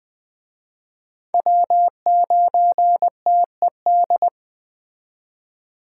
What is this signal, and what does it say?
Morse code, heard as a keyed tone, about 21 wpm, 700 Hz. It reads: W9TED
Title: Speed Racing - USA 1x3 Call Signs 17wpm